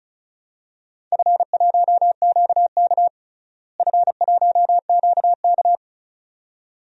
Morse code, heard as a keyed tone, about 36 wpm, 700 Hz. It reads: F1QK F1QK
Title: DXCC-100 Call Signs - 4 Characters - Repeated 2x - 35wpm